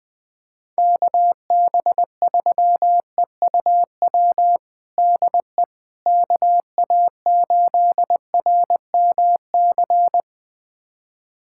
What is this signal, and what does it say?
Morse code, heard as a keyed tone, about 21 wpm, 700 Hz. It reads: KB3EUW DE KA8RMC
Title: USA Callsign DE Callsign - 6 Characters - 20wpm